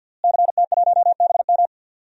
Morse code, heard as a keyed tone, about 51 wpm, 700 Hz. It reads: CT1BM
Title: DXCC-100 Call Signs - 5 Characters - Repeated 2x - 50wpm